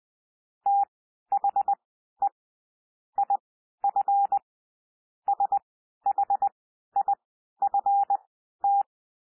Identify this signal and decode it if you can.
Morse code, heard as a keyed tone, about 20 wpm, spaced as for 12 wpm, 800 Hz. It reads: THE IF SHIFT
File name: cwmp3 / P1 SOUND PASSAGE 20 wpm